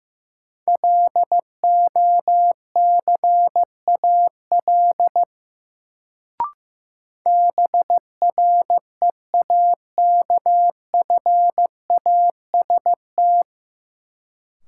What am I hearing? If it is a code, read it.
Morse code, heard as a keyed tone, about 15 wpm, 700 Hz. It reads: LOCAL BREAKFAST
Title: Sentences from Top 1500 Words - New Word Drill Alt - 15wpm